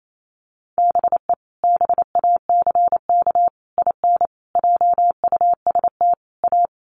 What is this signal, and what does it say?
Morse code, heard as a keyed tone, about 28 wpm, 700 Hz. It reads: BE BACK IN JUST A
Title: Sets of 5 Words Encore 28wpm